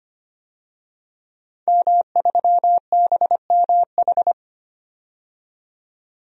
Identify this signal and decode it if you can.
Morse code, heard as a keyed tone, about 25 wpm, 700 Hz. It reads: M3BMH
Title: DXCC-100 Call Signs - 5 Characters - 25wpm